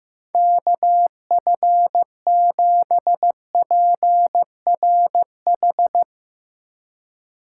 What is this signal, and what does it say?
Morse code, heard as a keyed tone, about 15 wpm, 700 Hz. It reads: KF7PRH
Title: USA Call Signs - 4x Character Spacing - 15wpm